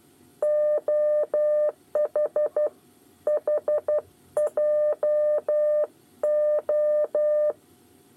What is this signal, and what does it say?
Morse code, heard as a keyed tone, about 12 wpm, 600 Hz. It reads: OHHJO
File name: 8 bis lezione cw 2